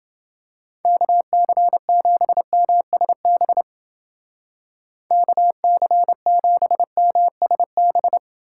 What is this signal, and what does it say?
Morse code, heard as a keyed tone, about 29 wpm, 700 Hz. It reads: KC7MSB KC7MSB